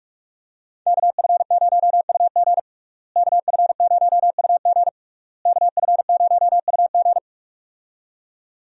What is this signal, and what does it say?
Morse code, heard as a keyed tone, about 46 wpm, 700 Hz. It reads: KF0UG KF0UG KF0UG
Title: USA Call Signs - 5 Characters - Repeated 3x - 45wpm